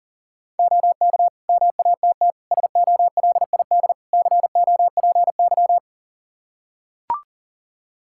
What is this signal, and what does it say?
Morse code, heard as a keyed tone, about 40 wpm, 700 Hz. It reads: OK MATT SOLID COPY
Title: Practice QSO - Exchange 3 - More Info - 40wpm